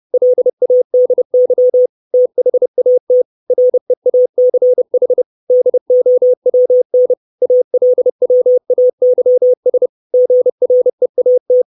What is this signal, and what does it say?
Morse code, heard as a keyed tone, about 30 wpm, 500 Hz. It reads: LADY THAT REACH DOWN ALWAYS GREAT
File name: 632 Words 30WPM